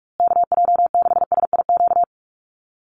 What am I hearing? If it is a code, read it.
Morse code, heard as a keyed tone, about 44 wpm, 700 Hz. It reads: KJ6SIQ
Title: USA Call Signs Repeated 2x 45wpm